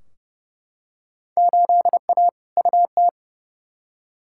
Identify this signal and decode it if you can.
Morse code, heard as a keyed tone, about 30 wpm, 700 Hz. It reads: 8A UT
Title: Field Day Exchanges Encore - 30wpm